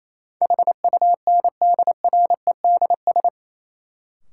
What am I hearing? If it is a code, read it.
Morse code, heard as a keyed tone, about 28 wpm, 700 Hz. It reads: HUNDREDS
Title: Sets of 4 Words - New Word Drill - 28wpm